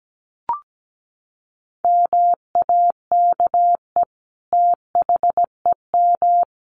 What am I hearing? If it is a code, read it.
Morse code, heard as a keyed tone, about 17 wpm, 700 Hz. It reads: MAKE THEM
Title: Sentences from Top 900 Words 17wpm